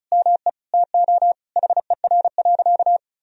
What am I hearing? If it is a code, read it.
Morse code, heard as a keyed tone, about 36 wpm, 700 Hz. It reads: ME TO HER.
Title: Sentences from Top 1000 Words - Review Part 2 - 35wpm